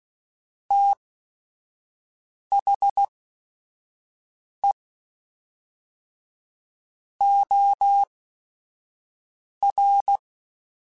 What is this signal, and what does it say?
Morse code, heard as a keyed tone, about 16 wpm, spaced as for 5 wpm, 800 Hz. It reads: THE OR